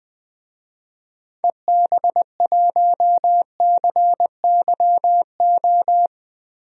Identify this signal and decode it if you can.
Morse code, heard as a keyed tone, about 20 wpm, 700 Hz. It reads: EB1CYO
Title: DXCC-100 Call Signs - 6 Characters - 20wpm